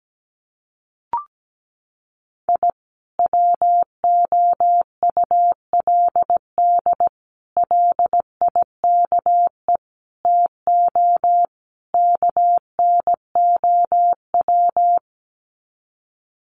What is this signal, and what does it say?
Morse code, heard as a keyed tone, about 17 wpm, 700 Hz. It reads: I WOULD LIKE TO KNOW
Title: Sets of 5 Words Encore 17wpm